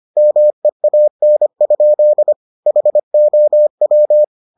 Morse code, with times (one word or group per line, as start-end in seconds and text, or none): 0.2-2.3 MEAN?
2.7-4.2 HOW